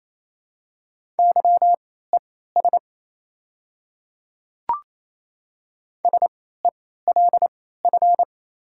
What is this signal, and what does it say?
Morse code, heard as a keyed tone, about 27 wpm, spaced as for 16 wpm, 700 Hz. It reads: YES SELF
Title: 3x Character Spacing - Top 500 Words - 28wpm